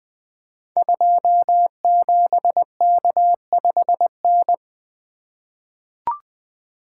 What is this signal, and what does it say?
Morse code, heard as a keyed tone, about 20 wpm, 700 Hz. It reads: 27K5N
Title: Code Groups - 5 Letter:Numbers - 20wpm